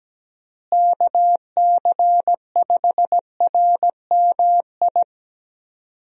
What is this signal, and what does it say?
Morse code, heard as a keyed tone, about 17 wpm, 700 Hz. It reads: KC5RMI
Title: USA Call Signs - 6 Characters - Repeated 3x - 17wpm